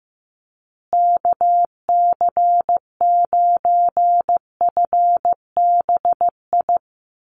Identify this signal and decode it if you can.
Morse code, heard as a keyed tone, about 15 wpm, 700 Hz. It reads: KC9FBI
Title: USA Call Signs Repeated 2x 15wpm